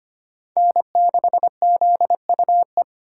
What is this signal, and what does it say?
Morse code, heard as a keyed tone, about 32 wpm, 700 Hz. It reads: N6ZUE